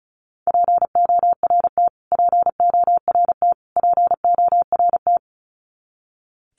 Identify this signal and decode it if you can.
Morse code, heard as a keyed tone, about 34 wpm, 700 Hz. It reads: PORT PORT PORT
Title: Top 300 Words Repeated 35wpm